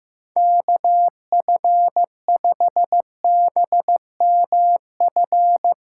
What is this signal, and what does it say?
Morse code, heard as a keyed tone, about 15 wpm, 700 Hz. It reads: KF5BMF
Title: USA Callsign DE Callsign - 6 Characters - 15wpm